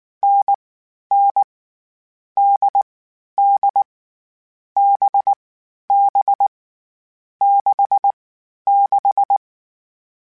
Morse code, heard as a keyed tone, about 19 wpm, 800 Hz. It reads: N N D D B B 6 6
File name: Difficult Combos / Difficult Combos-mp3s / NDB NDB6 @18